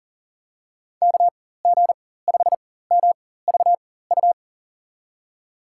Morse code, heard as a keyed tone, about 40 wpm, spaced as for 19 wpm, 700 Hz. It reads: KG5MVU
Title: USA Call Signs - 6 Characters - 4x Character Spacing - 40wpm